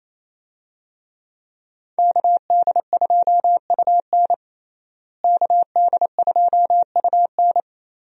Morse code, heard as a keyed tone, about 28 wpm, 700 Hz. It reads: KD2UN KD2UN
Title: USA Call Signs - 5 Characters - Repeated 3x - 28wpm